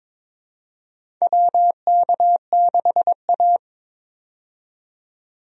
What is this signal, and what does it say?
Morse code, heard as a keyed tone, about 22 wpm, 700 Hz. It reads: WK6A
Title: USA Call Signs - 4 Characters - 22wpm